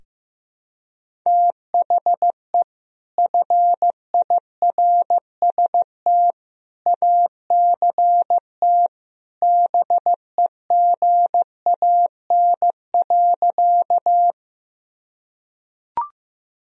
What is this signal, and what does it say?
Morse code, heard as a keyed tone, about 15 wpm, 700 Hz. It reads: THE FIRST ACT BEGAN.
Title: Sentences from Top 1000 Words - Review Part 1 - 15wpm